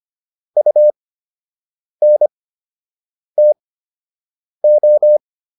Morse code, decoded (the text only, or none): UNTO